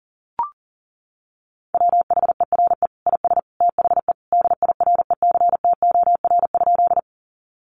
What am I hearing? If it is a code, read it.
Morse code, heard as a keyed tone, about 40 wpm, 700 Hz. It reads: WHERE IS THE DIRECTOR?
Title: Sentences from Top 1300 Words 40wpm